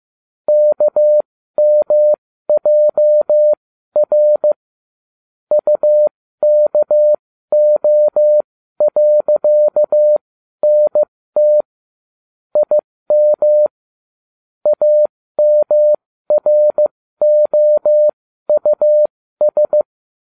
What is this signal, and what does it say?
Morse code, heard as a keyed tone, about 15 wpm, spaced as for 12 wpm, 600 Hz. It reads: KMJR UKO.NT IM AMROUS